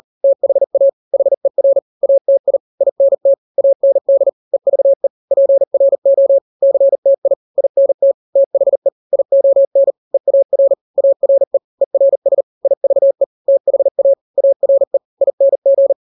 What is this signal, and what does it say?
Morse code, heard as a keyed tone, about 40 wpm, 550 Hz. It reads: THA HER ATI INT AND EVE PRO CTI INT THE ION EAR ARE ERS IVE THA ARE ING
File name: n-grams / 3-grams 40 35 WPM